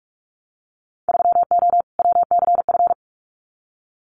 Morse code, heard as a keyed tone, about 44 wpm, 700 Hz. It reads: 3O WCF
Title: Winter Field Day Exchanges 45wpm